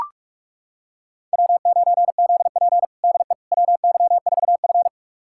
Morse code, heard as a keyed tone, about 44 wpm, 690 Hz. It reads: W9ZP DE WY4F
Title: USA Callsign DE Callsign - 4 Characters - 45wpm